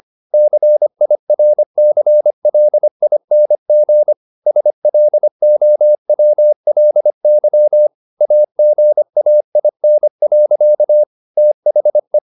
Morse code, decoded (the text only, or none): CIRCLING SLOWLY AGAIN. THE